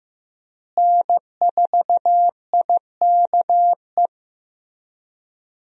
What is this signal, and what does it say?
Morse code, heard as a keyed tone, about 15 wpm, 700 Hz. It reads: N4IKE
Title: USA Call Signs - 5 Characters - Repeated 3x - 15wpm